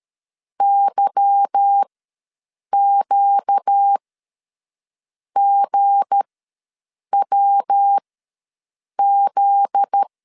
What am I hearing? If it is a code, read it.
Morse code, heard as a keyed tone, about 13 wpm, spaced as for 6 wpm, 800 Hz. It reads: YQGWZ
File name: LEZIONI / 8 YJ45 / 8 YJ ws 12wpm 3342